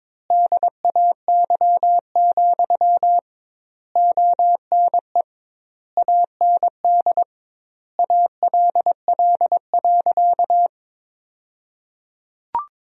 Morse code, heard as a keyed tone, about 22 wpm, 700 Hz. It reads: DAY, ONE AND ALL.
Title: 2x Word-Spacing - Sentences from Top 200 Words - 22wpm